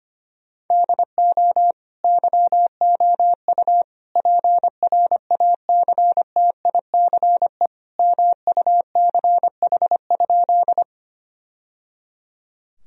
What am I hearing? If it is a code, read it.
Morse code, heard as a keyed tone, about 25 wpm, 700 Hz. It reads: DO YOU PRACTICE MUCH?
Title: Sentences from Top 1000 Words - Review Part 4 - 25wpm